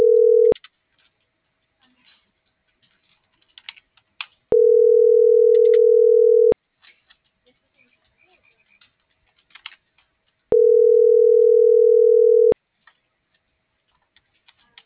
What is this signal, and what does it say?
Ringback tone (North America), 440 + 480 Hz.